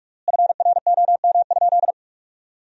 Morse code, heard as a keyed tone, about 46 wpm, 700 Hz. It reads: FROM?
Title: 3x Word-Spacing - Sentences from Top 300 Words - 45wpm